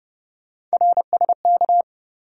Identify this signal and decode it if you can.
Morse code, heard as a keyed tone, about 32 wpm, 700 Hz. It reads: RSK
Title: USA Call Sign Postfixes - 3 Letters 30wpm